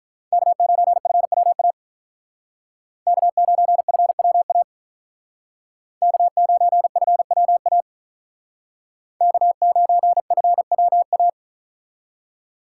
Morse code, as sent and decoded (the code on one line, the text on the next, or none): -.- ----. ..-. .-- .- / -.- ----. ..-. .-- .- / -.- ----. ..-. .-- .- / -.- ----. ..-. .-- .-
K9FWA K9FWA K9FWA K9FWA